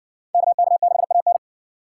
Morse code, heard as a keyed tone, about 54 wpm, 700 Hz. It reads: KK6AN